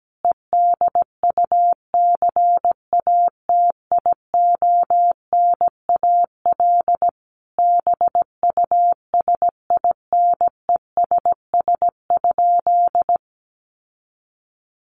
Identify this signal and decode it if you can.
Morse code, heard as a keyed tone, about 17 wpm, 700 Hz. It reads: EDUCATIONAL BUSINESS?
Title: Sentences from Top 1200 Words 17wpm